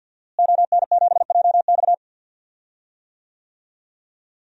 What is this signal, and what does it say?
Morse code, heard as a keyed tone, about 53 wpm, 700 Hz. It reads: ON7JX
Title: DXCC-100 Call Signs - 5 Characters - Repeated 3x - 50wpm